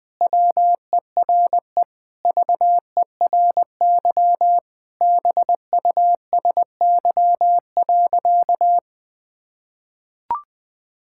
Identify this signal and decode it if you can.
Morse code, heard as a keyed tone, about 20 wpm, 700 Hz. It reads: WERE VERY BUSY.
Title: Sentences with 7-Word Phrases - 20wpm